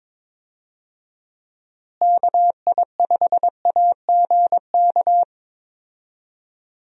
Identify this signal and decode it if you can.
Morse code, heard as a keyed tone, about 22 wpm, 700 Hz. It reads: KI5AGK
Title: USA Call Signs - 6 Characters - Repeated 2x - 22wpm